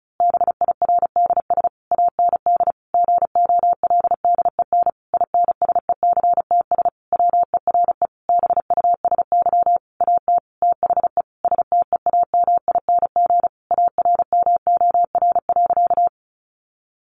Morse code, heard as a keyed tone, about 34 wpm, 700 Hz. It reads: BIRDS AND GOLDEN INSECTS WERE BUSY AT THE STEAMING ARMOR.
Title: A Sound Of Thunder 35wpm